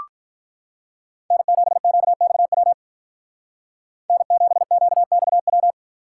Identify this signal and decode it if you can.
Morse code, heard as a keyed tone, about 44 wpm, 700 Hz. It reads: N7QXW N7QXW
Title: Speed Racing - USA 1x3 Call Signs 35wpm